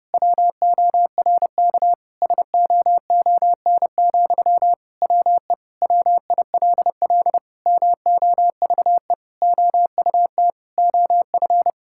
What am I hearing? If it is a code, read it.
Morse code, heard as a keyed tone, about 32 wpm, 700 Hz. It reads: WORK SOON, WE WILL MOVE OUT OF
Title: Sentences from Top 1000 Words - Review Part 6 - 30wpm